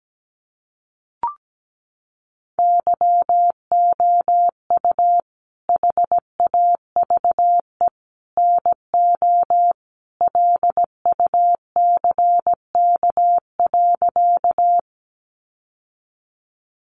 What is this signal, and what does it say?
Morse code, heard as a keyed tone, about 17 wpm, 700 Hz. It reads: YOU HAVE NO LUCK.
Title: Sentences from Top 1500 Words 17wpm